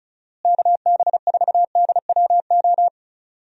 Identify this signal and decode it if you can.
Morse code, heard as a keyed tone, about 36 wpm, 700 Hz. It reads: KB4DWO
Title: USA Call Signs - 6 Characters - Repeated 2x - 35wpm